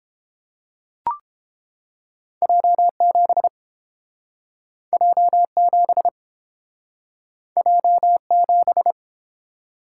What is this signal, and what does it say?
Morse code, heard as a keyed tone, about 29 wpm, 700 Hz. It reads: J7 J7 J7